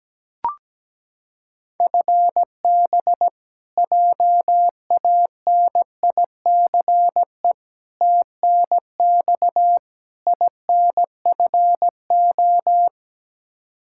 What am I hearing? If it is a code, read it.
Morse code, heard as a keyed tone, about 17 wpm, 700 Hz. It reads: FB JANICE TNX INFO
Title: K1USN SST QSO Practice 17wpm